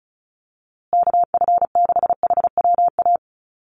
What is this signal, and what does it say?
Morse code, heard as a keyed tone, about 34 wpm, 700 Hz. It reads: KF6HWA
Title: USA Call Signs Repeated 2x 35wpm